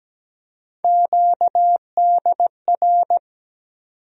Morse code, heard as a keyed tone, about 17 wpm, 700 Hz. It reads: QDR
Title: USA Call Sign Postfixes - 3 Letters 17wpm